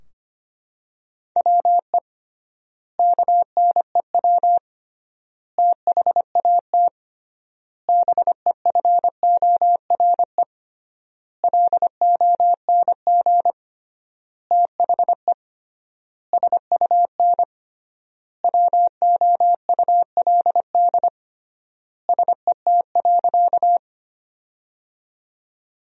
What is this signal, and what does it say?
Morse code, heard as a keyed tone, about 25 wpm, 700 Hz. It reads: WE KNEW THAT BEFORE LONG THE SUN WOULD SET.
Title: 3x Word-Spacing - Sentences from Top 400 Words - 25wpm